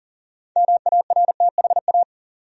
Morse code, heard as a keyed tone, about 40 wpm, 700 Hz. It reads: MARTHA
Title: USA Common Names - 40wpm